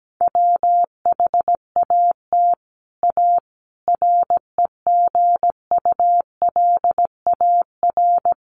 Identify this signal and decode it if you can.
Morse code, heard as a keyed tone, about 17 wpm, 700 Hz. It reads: WHAT A REGULAR
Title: Sentences from Top 900 Words 17wpm